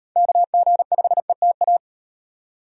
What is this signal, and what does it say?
Morse code, heard as a keyed tone, about 38 wpm, 700 Hz. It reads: KG5ETA